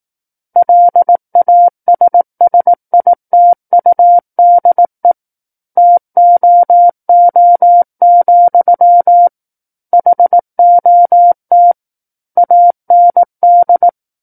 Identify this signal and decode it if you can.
Morse code, heard as a keyed tone, about 18 wpm, 700 Hz. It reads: LASSITUDE TOO, HOT AND
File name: 18 WPM